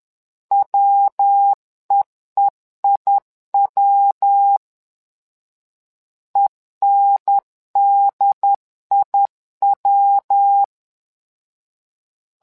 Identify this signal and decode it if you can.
Morse code, heard as a keyed tone, about 11 wpm, 800 Hz. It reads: WEEIW ENDIW
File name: LEZIONI / 3 MWRO / 3 MWRO 10wpm 2148